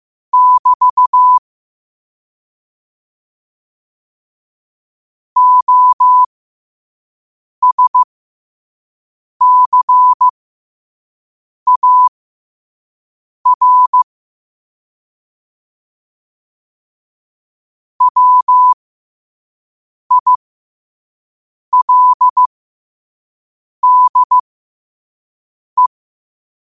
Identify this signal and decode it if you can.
Morse code, heard as a keyed tone, about 15 wpm, spaced as for 5 wpm, 1000 Hz. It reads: <BT> OSCAR WILDE
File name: qotdfiles / 5wpm